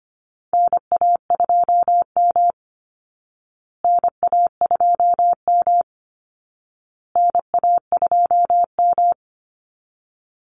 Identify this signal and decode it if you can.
Morse code, heard as a keyed tone, about 25 wpm, 700 Hz. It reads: NA2M NA2M NA2M
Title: Call Signs Repeated 2x - 25wpm